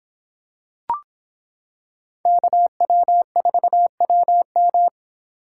Morse code, heard as a keyed tone, about 25 wpm, 700 Hz. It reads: KW4WM